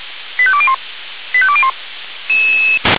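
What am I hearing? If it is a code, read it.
Five-tone selective call (ZVEI), heard as quick runs of five tones, tones of about 70 ms. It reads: 85201 85201
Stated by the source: ZVEI five-tone selective call received on an SDR.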